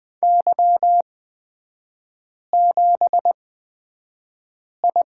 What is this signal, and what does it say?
Morse code, heard as a keyed tone, about 20 wpm, spaced as for 5 wpm, 700 Hz. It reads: Y7I
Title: 243 Lesson 39e - Letter z - Three Characters 20wpm